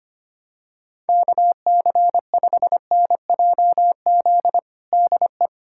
Morse code, heard as a keyed tone, about 25 wpm, 700 Hz. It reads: KC5NJZ DE
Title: USA Callsign DE Callsign 25wpm